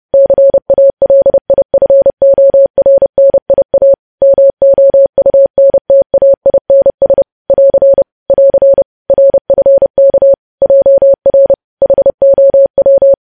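Morse code, heard as a keyed tone, about 30 wpm, 550 Hz. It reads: CALIFORNIA MOUNTAINS <AR> <AR> RFK JR HOW
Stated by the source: Morse code practice of news headlines at 30 wpm, preceded by vvv.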